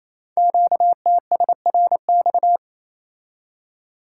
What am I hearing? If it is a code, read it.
Morse code, heard as a keyed tone, about 29 wpm, 700 Hz. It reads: QTSRX
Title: Code Groups - 5 Letters - 28wpm